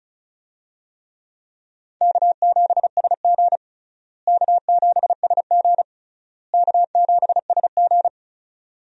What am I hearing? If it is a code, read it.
Morse code, heard as a keyed tone, about 36 wpm, 700 Hz. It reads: K7SG K7SG K7SG
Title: USA Call Signs - 4 Characters - Repeated 3x - 35wpm